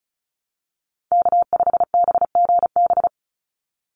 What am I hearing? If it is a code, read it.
Morse code, heard as a keyed tone, about 34 wpm, 700 Hz. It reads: K5BGB
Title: Call Signs Repeated 3x - 35wpm